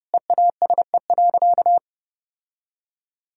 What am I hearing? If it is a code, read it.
Morse code, heard as a keyed tone, about 30 wpm, 700 Hz. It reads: EASE.
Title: Sentences from Top 1000 Words - Review Part 1 - 30wpm